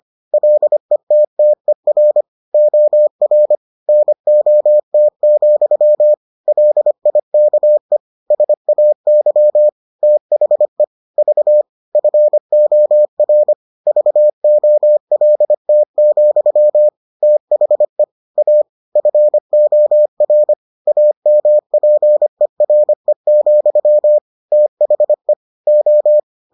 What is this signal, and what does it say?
Morse code, heard as a keyed tone, about 25 wpm, 600 Hz. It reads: LETTER OR NOT, LIKE SAY THE V FOR VOLT, THE A FOR AMPERE, THE O